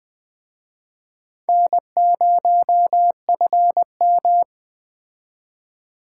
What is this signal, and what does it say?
Morse code, heard as a keyed tone, about 20 wpm, 700 Hz. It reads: N0FM